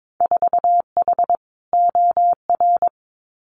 Morse code, heard as a keyed tone, about 22 wpm, 700 Hz. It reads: 4H OR
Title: Winter Field Day Exchanges Encore 22wpm